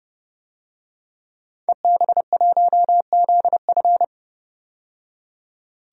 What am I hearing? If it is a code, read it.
Morse code, heard as a keyed tone, about 29 wpm, 700 Hz. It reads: EB1ZF
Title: DXCC-100 Call Signs - 5 Characters - 30wpm